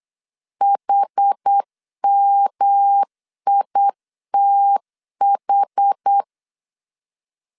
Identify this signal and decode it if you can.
Morse code, heard as a keyed tone, about 8 wpm, 800 Hz. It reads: HMITH
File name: CW / DET / 1 3 1500C S B 8wpm 4141